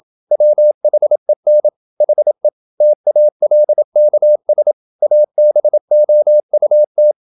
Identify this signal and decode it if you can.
Morse code, heard as a keyed tone, about 27 wpm, 600 Hz. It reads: WHEN HE TALKS ABOUT
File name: Never 0027wpm